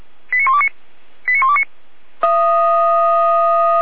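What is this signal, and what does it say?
Five-tone selective call (ZVEI), heard as quick runs of five tones, tones of about 70 ms. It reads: 89129 89129
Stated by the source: ZVEI five-tone selective call 89130 received by SDR.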